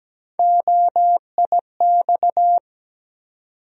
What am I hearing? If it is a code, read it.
Morse code, heard as a keyed tone, about 17 wpm, 700 Hz. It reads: OIX
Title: USA Call Sign Postfixes - 3 Letters 17wpm